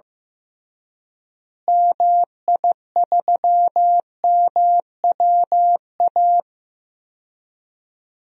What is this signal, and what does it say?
Morse code, heard as a keyed tone, about 15 wpm, 700 Hz. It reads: MI3MWA
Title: DXCC-100 Call Signs - 6 Characters - Repeated 3x - 15wpm